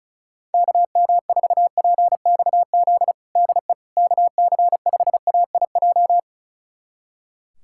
Morse code, heard as a keyed tone, about 36 wpm, 700 Hz. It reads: KM4PXZ DE KC5AIJ
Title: USA Callsign DE Callsign - 6 Characters - 35wpm